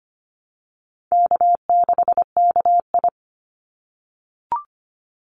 Morse code, heard as a keyed tone, about 25 wpm, 700 Hz. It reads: K6KI
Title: Easy Call Signs - 25wpm